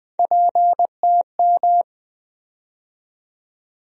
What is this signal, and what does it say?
Morse code, heard as a keyed tone, about 20 wpm, 700 Hz. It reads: PTM
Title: USA Call Sign Postfixes - 3 Letters 20wpm